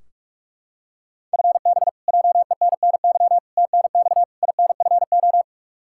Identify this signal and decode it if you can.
Morse code, heard as a keyed tone, about 44 wpm, 700 Hz. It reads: FB JENNY TNX INFO
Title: K1USN SST QSO Practice 45wpm